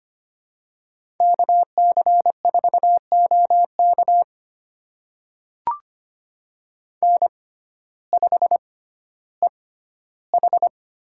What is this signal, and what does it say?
Morse code, heard as a keyed tone, about 25 wpm, spaced as for 14 wpm, 700 Hz. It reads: KC4OK N5EH
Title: USA Call Signs - 6x Character Spacing - 25wpm